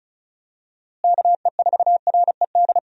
Morse code, heard as a keyed tone, about 36 wpm, 700 Hz. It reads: KE4RED
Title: USA Call Signs - 6 Characters - Repeated 3x - 35wpm